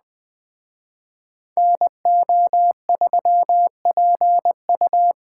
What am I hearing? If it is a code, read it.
Morse code, heard as a keyed tone, about 20 wpm, 700 Hz. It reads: NO3PU
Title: Speed Racing - USA 2x2 Call Signs 17wpm